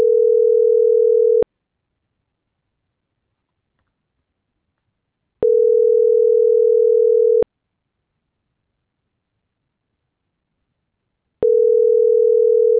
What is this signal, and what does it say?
Ringback tone (North America), 440 + 480 Hz.
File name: CENTRAL MABE / out manuel.posadas - .48583